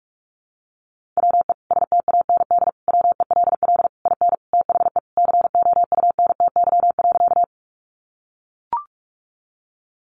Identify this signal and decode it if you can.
Morse code, heard as a keyed tone, about 44 wpm, 700 Hz. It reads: WE STAND WELL IN THE COUNTY.
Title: Sentences from Top 1200 Words 45wpm